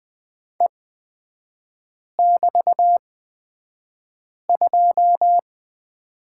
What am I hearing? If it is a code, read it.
Morse code, heard as a keyed tone, about 20 wpm, spaced as for 5 wpm, 700 Hz. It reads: E<BT>2
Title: 224 Lesson 36e - BT - Three Characters 20wpm